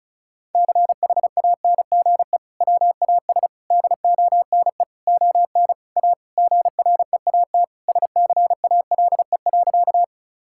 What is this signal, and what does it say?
Morse code, heard as a keyed tone, about 36 wpm, 700 Hz. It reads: CHANGE WAS DONE ON A GREAT SCALE.
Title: Sentences from Top 1000 Words - Review Part 5 - 35wpm